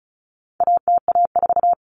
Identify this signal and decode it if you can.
Morse code, heard as a keyed tone, about 34 wpm, 700 Hz. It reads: ATA4
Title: Common Cut Numbers - 4-Digit 35wpm